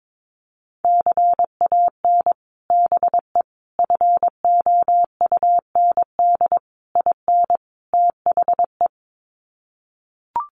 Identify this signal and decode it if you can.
Morse code, heard as a keyed tone, about 22 wpm, 700 Hz. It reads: CAN BE FOUND IN THE
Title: Sets of 5 Words Encore 22wpm